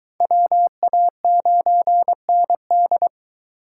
Morse code, heard as a keyed tone, about 22 wpm, 700 Hz. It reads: WA9ND